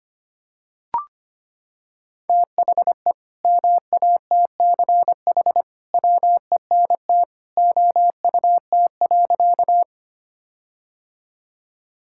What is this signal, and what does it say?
Morse code, heard as a keyed tone, about 25 wpm, 700 Hz. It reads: THE MATCH WENT OUT.
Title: Sentences from Top 1600 Words 25wpm